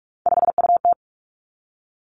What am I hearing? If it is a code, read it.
Morse code, heard as a keyed tone, about 46 wpm, 700 Hz. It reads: 5UT
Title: Three Challenging Letter-Number Combinations 45wpm